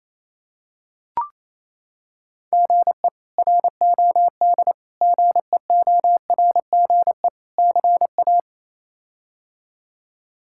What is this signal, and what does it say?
Morse code, heard as a keyed tone, about 27 wpm, 700 Hz. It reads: GE ROD GEORGE CA
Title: K1USN SST QSO Practice - 28wpm